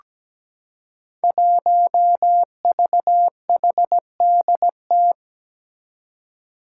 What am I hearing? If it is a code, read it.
Morse code, heard as a keyed tone, about 17 wpm, 700 Hz. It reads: 1VHDT
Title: Five Challenging Letter-Number Combinations 17wpm